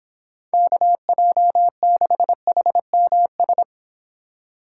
Morse code, heard as a keyed tone, about 26 wpm, 700 Hz. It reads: KJ6HMS